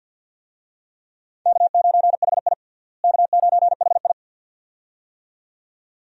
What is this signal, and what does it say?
Morse code, heard as a keyed tone, about 50 wpm, 700 Hz. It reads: K9HI K9HI
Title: USA Call Signs - 4 Characters - Repeated 2x - 50wpm